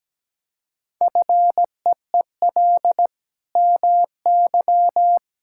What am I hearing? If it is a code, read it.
Morse code, heard as a keyed tone, about 17 wpm, 700 Hz. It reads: FEEL MY
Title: Sentences from Top 1000 Words - Review Part 5 - 17wpm